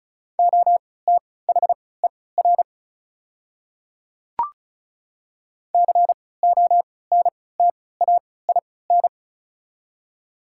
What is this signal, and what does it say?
Morse code, heard as a keyed tone, about 36 wpm, spaced as for 20 wpm, 700 Hz. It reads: OTHER CONTAIN
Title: 3x Character Spacing - Top 500 Words - 35wpm